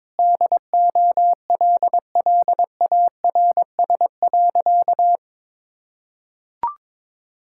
Morse code, heard as a keyed tone, about 22 wpm, 700 Hz. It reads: DOLLARS.
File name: Sentences with 9-Word Phrases - 22wpm